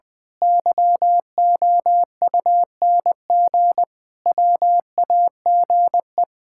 Morse code, heard as a keyed tone, about 20 wpm, 700 Hz. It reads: YOUNG WAGE
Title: Pangram Challenge v2 - 20wpm